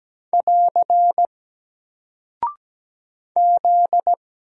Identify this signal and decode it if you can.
Morse code, heard as a keyed tone, about 17 wpm, 700 Hz. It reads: <AR> Z
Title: Single Character (Letter:Number:Punctuation:Prosign) Alt 17wpm